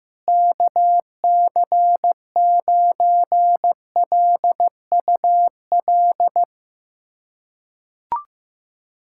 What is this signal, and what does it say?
Morse code, heard as a keyed tone, about 15 wpm, 700 Hz. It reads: KC9LUL
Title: USA Callsign DE Callsign - 6 Characters - 15wpm